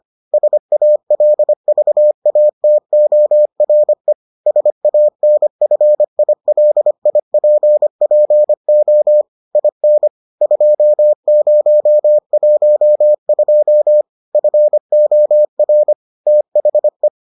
Morse code, heard as a keyed tone, about 25 wpm, 600 Hz. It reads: SALVATORE SANFILIPPO IN 2012 FOR THE